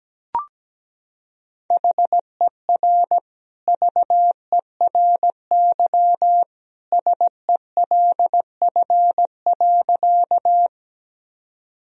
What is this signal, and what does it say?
Morse code, heard as a keyed tone, about 17 wpm, 700 Hz. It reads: HER VERY SELF.
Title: Sentences from Top 1000 Words - Review Part 1 - 17wpm